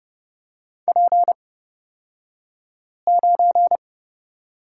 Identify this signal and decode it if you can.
Morse code, heard as a keyed tone, about 31 wpm, spaced as for 5 wpm, 700 Hz. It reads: P9
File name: Lesson 38d - Letter q - Two Characters 30wpm